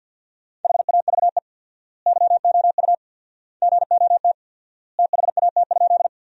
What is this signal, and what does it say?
Morse code, heard as a keyed tone, about 50 wpm, 700 Hz. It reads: HAVE YOU GOT THAT?
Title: 4x Word-Spacing - Sentences from Top 300 Words - 50wpm